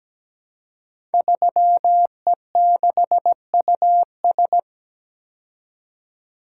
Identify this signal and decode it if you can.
Morse code, heard as a keyed tone, about 17 wpm, 700 Hz. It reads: 3E6US
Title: Code Groups - 5 Letter:Numbers - 17wpm